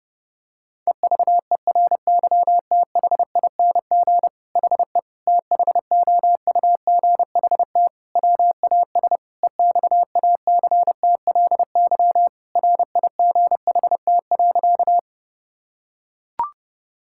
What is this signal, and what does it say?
Morse code, heard as a keyed tone, about 30 wpm, 700 Hz. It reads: EVERYTHING HE THOUGHT WAS EXACTLY RIGHT.
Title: Sentences from Top 1000 Words - Review Part 6 - 30wpm